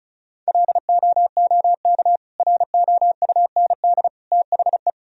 Morse code, heard as a keyed tone, about 34 wpm, 700 Hz. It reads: LOOK ROUND THE
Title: Sentences from Top 1000 Words - Review Part 1 - 35wpm